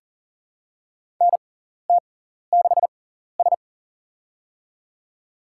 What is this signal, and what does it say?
Morse code, heard as a keyed tone, about 40 wpm, spaced as for 14 wpm, 700 Hz. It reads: NT6S
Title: USA Call Signs - 4 Characters - 6x Character Spacing - 40wpm